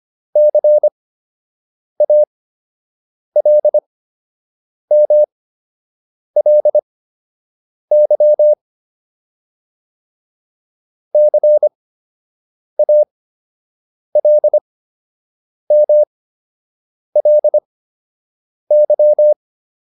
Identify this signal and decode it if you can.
Morse code, heard as a keyed tone, about 25 wpm, spaced as for 7 wpm, 600 Hz. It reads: CALMLY CALMLY
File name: LY6 07